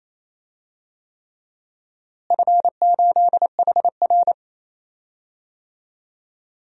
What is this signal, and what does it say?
Morse code, heard as a keyed tone, about 28 wpm, 700 Hz. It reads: F8HR
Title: DXCC-100 Call Signs - 4 Characters - Repeated 3x - 28wpm